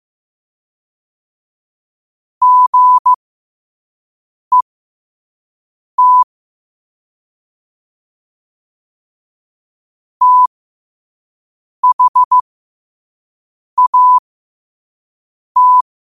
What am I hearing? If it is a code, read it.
Morse code, heard as a keyed tone, about 15 wpm, spaced as for 5 wpm, 1000 Hz. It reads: GET THAT
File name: qotdfiles / 5wpm